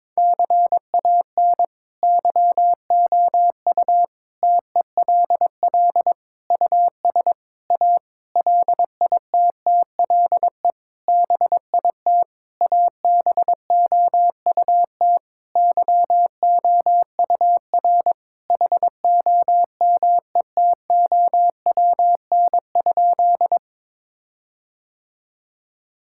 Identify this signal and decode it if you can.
Morse code, heard as a keyed tone, about 22 wpm, 700 Hz. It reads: CAN YOU TELL US A LITTLE BIT ABOUT YOUR HOMETOWN?
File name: practice / 22wpm / Sentences with 8-Word Phrases - 22wpm